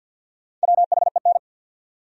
Morse code, heard as a keyed tone, about 50 wpm, 700 Hz. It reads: WHEN
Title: Top 100 Words 50wpm